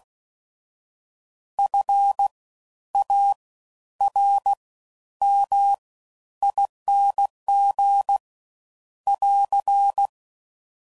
Morse code, heard as a keyed tone, about 16 wpm, spaced as for 11 wpm, 800 Hz. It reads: FARMING <AR>
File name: Suffixes / Special INGs / Fast Ings